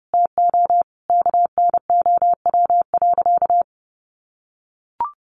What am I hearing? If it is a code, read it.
Morse code, heard as a keyed tone, about 30 wpm, 700 Hz. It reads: TO KNOW.
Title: Sentences from Top 600 Words 30wpm